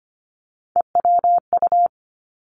Morse code, heard as a keyed tone, about 25 wpm, 700 Hz. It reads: EWU
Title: Three Challenging Letter-Number Combinations 25wpm